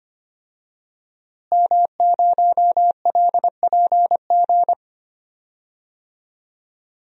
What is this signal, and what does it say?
Morse code, heard as a keyed tone, about 25 wpm, 700 Hz. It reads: M0LPG
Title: DXCC-100 Call Signs - 5 Characters - Repeated 2x - 25wpm